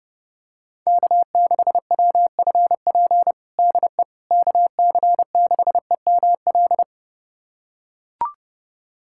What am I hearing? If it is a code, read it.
Morse code, heard as a keyed tone, about 30 wpm, 700 Hz. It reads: K6WFP DE KC6EML
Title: USA Callsign DE Callsign 30wpm